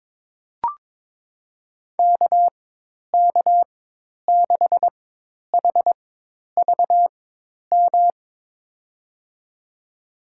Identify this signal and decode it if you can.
Morse code, heard as a keyed tone, about 22 wpm, spaced as for 10 wpm, 700 Hz. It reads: KK6HVM
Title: USA Call Signs - 6 Characters - 4x Character Spacing - 22wpm